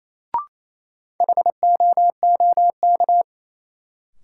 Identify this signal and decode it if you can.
Morse code, heard as a keyed tone, about 28 wpm, 700 Hz. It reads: HOOK
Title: Sentences from Top 1600 Words - New Word Drill - 28wpm